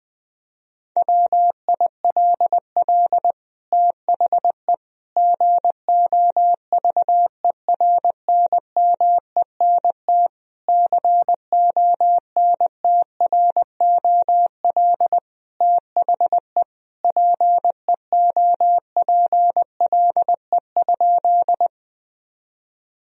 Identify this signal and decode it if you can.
Morse code, heard as a keyed tone, about 21 wpm, 700 Hz. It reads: WILL THE GOVERNMENT CONTROL THE PEOPLE?
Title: Sentences from Top 1000 Words - Review Part 3 - 20wpm